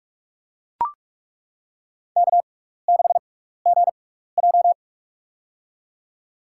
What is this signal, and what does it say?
Morse code, heard as a keyed tone, about 41 wpm, spaced as for 15 wpm, 700 Hz. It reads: K6GJ
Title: USA Call Signs - 4 Characters - 6x Character Spacing - 45wpm